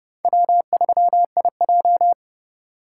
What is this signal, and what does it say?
Morse code, heard as a keyed tone, about 29 wpm, 700 Hz. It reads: W3IJ